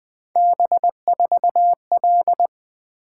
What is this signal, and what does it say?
Morse code, heard as a keyed tone, about 20 wpm, 700 Hz. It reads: B4L
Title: Three Letter-Number Combinations - 20wpm